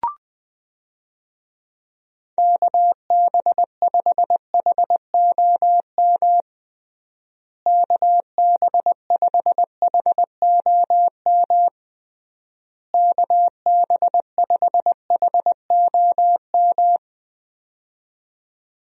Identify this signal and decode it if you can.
Morse code, heard as a keyed tone, about 21 wpm, 700 Hz. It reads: KB5HOM KB5HOM KB5HOM
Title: USA Call Signs - 6 Characters - Repeated 3x - 20wpm